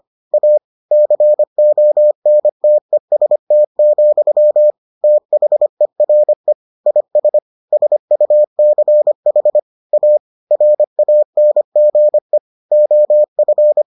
Morse code, heard as a keyed tone, about 25 wpm, 600 Hz. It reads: A CONTEST, THERE IS SUCH A RANGE OF